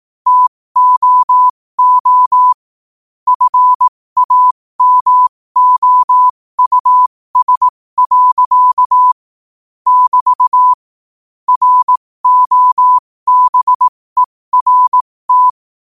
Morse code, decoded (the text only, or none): TOO FAMOUS. <BT> ROBERT